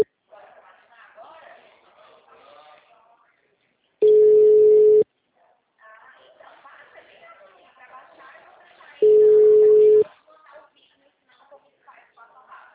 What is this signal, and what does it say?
Ringback tone (Europe, 425 Hz), with speech.